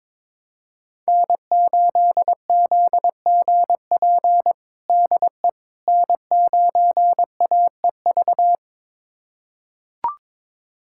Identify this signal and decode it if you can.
Morse code, heard as a keyed tone, about 22 wpm, 700 Hz. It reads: N8ZGP DE N9AEV
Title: USA Callsign DE Callsign - 5 Characters - 22wpm